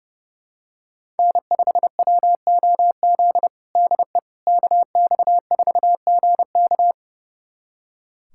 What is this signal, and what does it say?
Morse code, heard as a keyed tone, about 30 wpm, 700 Hz. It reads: N5WOZ DE KX4GK
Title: USA Callsign DE Callsign - 5 Characters - 30wpm